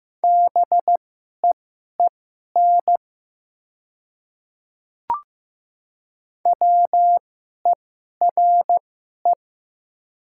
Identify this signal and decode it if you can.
Morse code, heard as a keyed tone, about 15 wpm, spaced as for 11 wpm, 700 Hz. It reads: BEEN WERE
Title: 2x Character Spacing - Top 100 Words - 15wpm